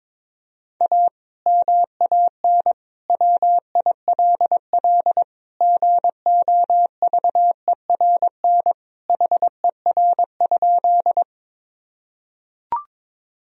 Morse code, heard as a keyed tone, about 22 wpm, 700 Hz. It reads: A MAN WILL GOVERN HER?
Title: Sentences from Top 1000 Words - Review Part 2 - 22wpm